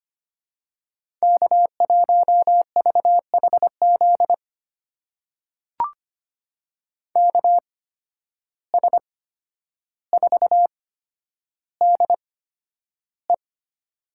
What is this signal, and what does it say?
Morse code, heard as a keyed tone, about 25 wpm, spaced as for 11 wpm, 700 Hz. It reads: K1VHZ KS4DE
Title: USA Call Signs - 5 Characters - 8x Character Spacing - 25wpm